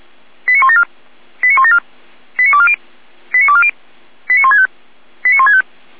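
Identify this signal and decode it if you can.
Five-tone selective call (ZVEI), heard as quick runs of five tones, tones of about 70 ms. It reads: 89174 89174 89240 89240 89165 89165
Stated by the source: ZVEI five-tone selective call received on an SDR.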